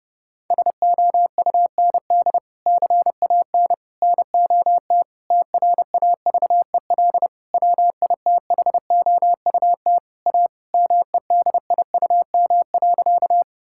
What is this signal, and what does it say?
Morse code, heard as a keyed tone, about 30 wpm, 700 Hz. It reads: SOUND CAN NOT TRAVEL WITHOUT A MEDIUM.
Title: Sentences from Top 1000 Words - Review Part 6 - 30wpm